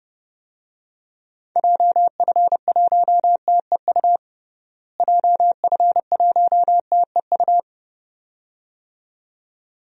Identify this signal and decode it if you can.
Morse code, heard as a keyed tone, about 29 wpm, 700 Hz. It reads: JF1TEU JF1TEU
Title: DXCC-100 Call Signs - 6 Characters - Repeated 2x - 30wpm